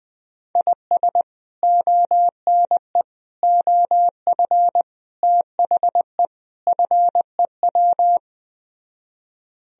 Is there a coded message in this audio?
Yes: Morse code from 0.6 to 8.2 s, heard as a keyed tone, about 20 wpm, 700 Hz. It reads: IS ONE OF THE FEW